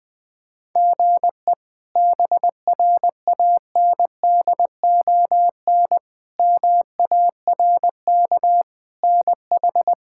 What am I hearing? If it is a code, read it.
Morse code, heard as a keyed tone, about 20 wpm, 700 Hz. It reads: GE BRANDON MARK NH
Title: K1USN SST QSO Practice - 20wpm